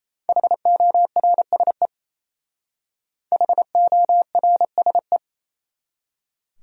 Morse code, heard as a keyed tone, about 35 wpm, 700 Hz. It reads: HORSE HORSE
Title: Speed-Racing Top 300 Words 28wpm